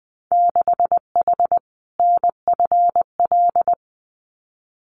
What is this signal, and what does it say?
Morse code, heard as a keyed tone, about 20 wpm, 700 Hz. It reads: 6H NFL
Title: Winter Field Day Exchanges Encore 20wpm